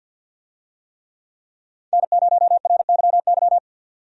Morse code, heard as a keyed tone, about 53 wpm, 700 Hz. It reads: N0RYY
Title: USA Call Signs - 5 Characters - 50wpm